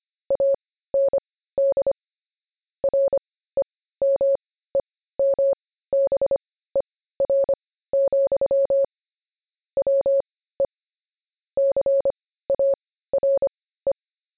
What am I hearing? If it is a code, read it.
Morse code, heard as a keyed tone, about 25 wpm, spaced as for 15 wpm, 550 Hz. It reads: AND REMEMBER, WE CARE
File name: jb recaptured / morse